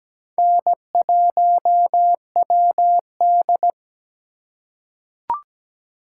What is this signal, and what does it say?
Morse code, heard as a keyed tone, about 17 wpm, 700 Hz. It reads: N1WD
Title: USA Call Signs - 4 Characters - 3x Character Spacing - 17wpm